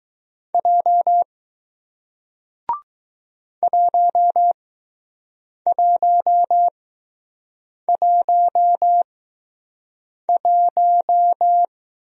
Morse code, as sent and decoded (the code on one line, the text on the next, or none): .--- / .---- .---- .---- .----
J 1111